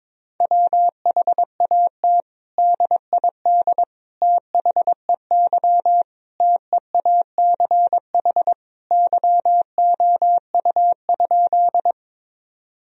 Morse code, heard as a keyed tone, about 21 wpm, 700 Hz. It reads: WHAT DID THEY TEACH YOU?
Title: Sentences from Top 1000 Words - Review Part 2 - 22wpm